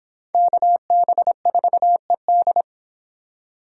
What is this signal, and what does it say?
Morse code, heard as a keyed tone, about 29 wpm, 700 Hz. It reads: KB4ED